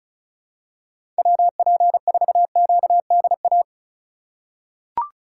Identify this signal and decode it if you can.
Morse code, heard as a keyed tone, about 34 wpm, 700 Hz. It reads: WP4QDA
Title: USA Call Signs - 6 Characters - 2x Character Spacing - 35wpm